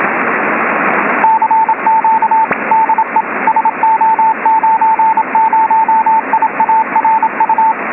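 Morse code, heard as a keyed tone, about 27 wpm, 890 Hz, with radio noise. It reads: CQ DE SO90IARU
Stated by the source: Morse signal on 28025 kHz from SO90IARU.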